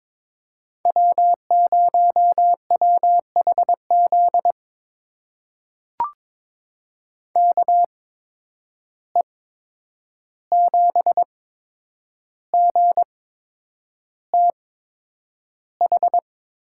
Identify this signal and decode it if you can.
Morse code, heard as a keyed tone, about 23 wpm, spaced as for 9 wpm, 700 Hz. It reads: W0WHZ KE7GTH
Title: USA Call Signs - 8x Character Spacing - 22wpm